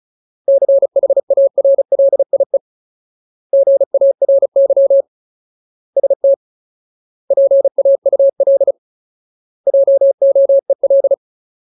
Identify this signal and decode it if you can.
Morse code, heard as a keyed tone, about 34 wpm, 550 Hz. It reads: CHARLIE GARY ST PAUL JOEL
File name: Ultimate Word List / 35WPM Ultimate